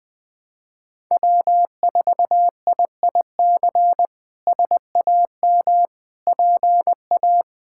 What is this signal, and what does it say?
Morse code, heard as a keyed tone, about 20 wpm, 700 Hz. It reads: W4IIC SAM PA
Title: K1USN SST QSO Practice - 20wpm